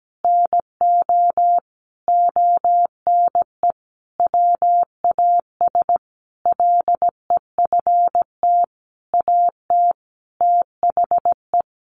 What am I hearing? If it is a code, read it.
Morse code, heard as a keyed tone, about 17 wpm, 700 Hz. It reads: NO ONE WAS LEFT AT THE
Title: Sentences from Top 1400 Words 17wpm